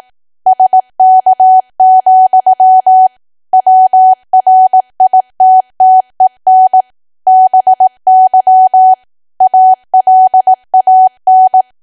Morse code, heard as a keyed tone, about 18 wpm, 750 Hz. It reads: SK, WRITTEN BY ALAN
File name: Archive / 18 WPM / 18WPM